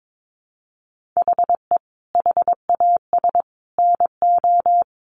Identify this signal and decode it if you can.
Morse code, heard as a keyed tone, about 22 wpm, 700 Hz. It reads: HE HAS NO
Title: Sentences from Top 1500 Words 22wpm